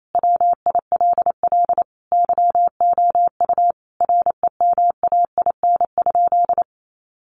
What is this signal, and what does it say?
Morse code, heard as a keyed tone, about 28 wpm, 700 Hz. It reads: WILL YOU REMAIN?
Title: Sentences from Top 1200 Words 28wpm